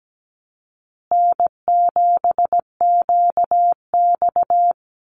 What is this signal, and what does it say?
Morse code, heard as a keyed tone, about 17 wpm, 700 Hz. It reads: N7QX
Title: USA Call Signs Repeated 3x 17wpm